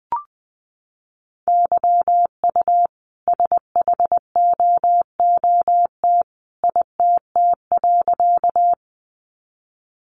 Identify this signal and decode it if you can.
Morse code, heard as a keyed tone, about 20 wpm, 700 Hz. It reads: YU SHOOT ITT.
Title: A Sound Of Thunder 20wpm